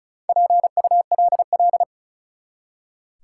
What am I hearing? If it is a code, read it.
Morse code, heard as a keyed tone, about 34 wpm, 700 Hz. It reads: PULL
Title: Four Letter Words - 35wpm